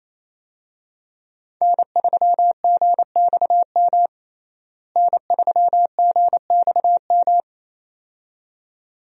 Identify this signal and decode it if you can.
Morse code, heard as a keyed tone, about 27 wpm, 700 Hz. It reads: N3GXM N3GXM
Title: USA Call Signs - 5 Characters - Repeated 2x - 28wpm